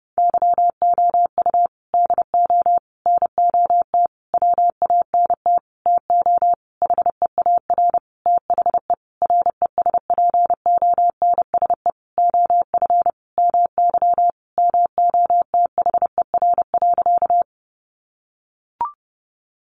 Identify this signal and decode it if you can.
Morse code, heard as a keyed tone, about 28 wpm, 700 Hz. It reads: YOU DO NOT WANT TO HEAR THE RESPONSE OF MY MOTHER.
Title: Sentences from Top 1400 Words 30wpm